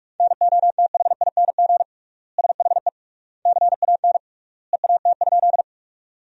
Morse code, heard as a keyed tone, about 47 wpm, 700 Hz. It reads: NOTHING SHE CAN EAT?
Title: 3x Word-Spacing - Sentences from Top 500 Words - 45wpm